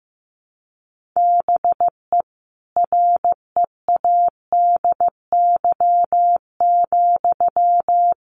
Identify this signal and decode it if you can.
Morse code, heard as a keyed tone, about 15 wpm, 700 Hz. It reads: BE READY,
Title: Sentences from Top 1400 Words 15wpm